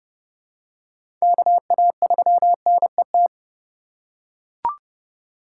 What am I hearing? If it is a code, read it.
Morse code, heard as a keyed tone, about 31 wpm, 700 Hz. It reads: KA3NET
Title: USA Call Signs - 6 Characters - 4x Character Spacing - 30wpm